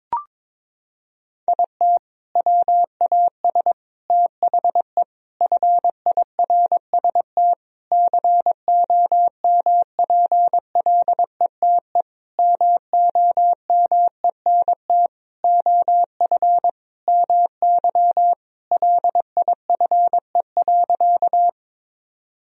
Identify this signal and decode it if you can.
Morse code, heard as a keyed tone, about 22 wpm, 700 Hz. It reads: IT WAS THE FIRST COMPLETE MOMENT OF MY LIFE.
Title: Sentences from Top 1000 Words - Review Part 2 - 22wpm